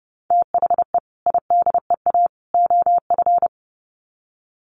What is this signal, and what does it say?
Morse code, heard as a keyed tone, about 30 wpm, 700 Hz. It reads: THE IDEA OF
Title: Sets of 3 Words 30wpm